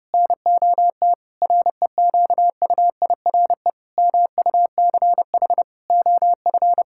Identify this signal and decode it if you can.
Morse code, heard as a keyed tone, about 30 wpm, 700 Hz. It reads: NOT REQUIRE MUCH OF
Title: Sentences from Top 1000 Words - Review Part 6 - 30wpm